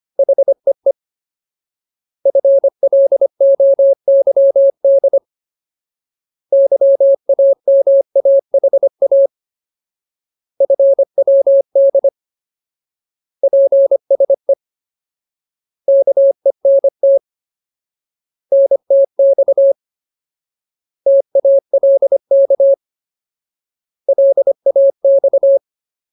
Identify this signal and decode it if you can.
Morse code, heard as a keyed tone, about 25 wpm, 550 Hz. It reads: HEE FLOYD YAMAHA FWD PSE KENT NTX TALK LAX
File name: Ultimate Word List / 25WPM Ultimate